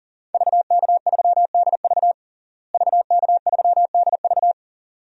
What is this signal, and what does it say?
Morse code, heard as a keyed tone, about 40 wpm, 700 Hz. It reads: VK3DV VK3DV
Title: DXCC-100 Call Signs - 5 Characters - Repeated 2x - 40wpm